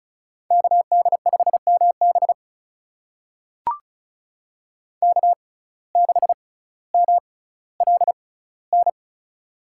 Morse code, heard as a keyed tone, about 36 wpm, spaced as for 18 wpm, 700 Hz. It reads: KD5MB K6MLN
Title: USA Call Signs - 5 Characters - 6x Character Spacing - 35wpm